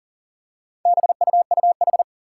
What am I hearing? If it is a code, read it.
Morse code, heard as a keyed tone, about 37 wpm, 700 Hz. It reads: BUUH
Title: Four Challenging Letter-Number Combinations 40wpm